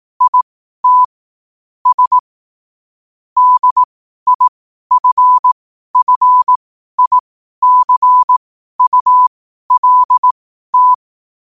Morse code, heard as a keyed tone, about 18 wpm, spaced as for 12 wpm, 1000 Hz. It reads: IT S DIFFICULT
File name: qotdfiles / 12wpm